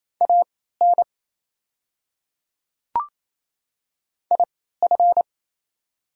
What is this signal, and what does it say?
Morse code, heard as a keyed tone, about 28 wpm, spaced as for 16 wpm, 690 Hz. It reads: AN IF
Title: 3x Character Spacing - Top 100 Words - 28wpm